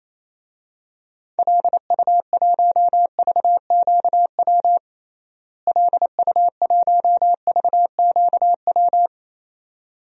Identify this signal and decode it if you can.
Morse code, heard as a keyed tone, about 28 wpm, 700 Hz. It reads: LU1VQW LU1VQW
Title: DXCC-100 Call Signs - 6 Characters - Repeated 2x - 28wpm